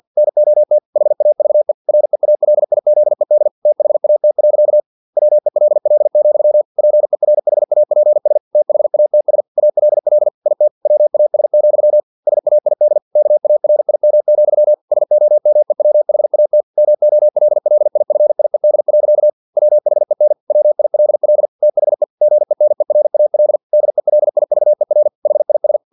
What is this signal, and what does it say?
Morse code, heard as a keyed tone, about 50 wpm, 600 Hz. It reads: NOT HAVE REALIZED THAT. WELL, PERHAPS THATS ALL IT WAS, SAID KARIM, SOMEWHAT MOLLIFIED. WHEN WILL THE GENERAL DELIVER HIS
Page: Never 49wpm